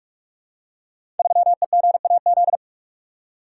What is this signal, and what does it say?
Morse code, heard as a keyed tone, about 46 wpm, 700 Hz. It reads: 3EGAZ
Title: Code Groups - 5 Letter:Numbers - 45wpm